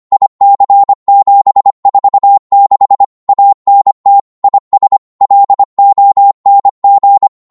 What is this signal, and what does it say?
Morse code, heard as a keyed tone, about 25 wpm, 800 Hz. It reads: IC746 ANT IS LONG
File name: Morse / qso-25wpm